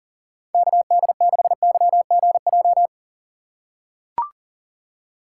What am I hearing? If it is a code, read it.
Morse code, heard as a keyed tone, about 39 wpm, 700 Hz. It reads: KD6YGJ
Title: USA Call Signs - 6 Characters - 4x Character Spacing - 40wpm